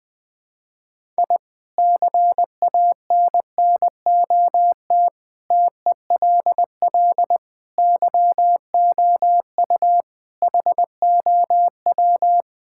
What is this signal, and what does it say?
Morse code, heard as a keyed tone, about 21 wpm, 700 Hz. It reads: I CANNOT TELL YOU HOW
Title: Sentences with 5-Word Phrases - 20wpm